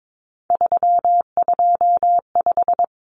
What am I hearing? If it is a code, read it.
Morse code, heard as a keyed tone, about 22 wpm, 700 Hz. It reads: 325
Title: Common Cut Numbers - 3-Digit 22wpm